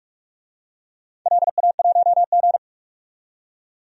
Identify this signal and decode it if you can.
Morse code, heard as a keyed tone, about 44 wpm, 700 Hz. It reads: LA1G
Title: DXCC-100 Call Signs - 4 Characters - 45wpm